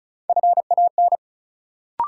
Morse code, heard as a keyed tone, about 36 wpm, 700 Hz. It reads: FAN
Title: Sentences from Top 1300 Words - New Word Drill Alt - 35wpm